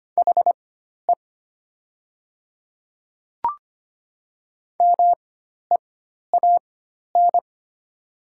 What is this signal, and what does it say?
Morse code, heard as a keyed tone, about 25 wpm, spaced as for 12 wpm, 700 Hz. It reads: HE MEAN
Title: 4x Character Spacing - Top 200 Words - 25wpm